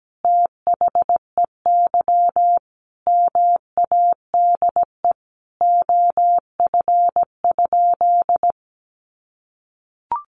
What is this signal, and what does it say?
Morse code, heard as a keyed tone, about 17 wpm, 700 Hz. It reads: THEY MADE OF?
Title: Sentences from Top 200 Words 17wpm